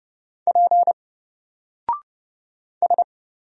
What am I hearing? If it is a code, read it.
Morse code, heard as a keyed tone, about 30 wpm, 700 Hz. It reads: P S
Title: Single Letter-Number Alt 30wpm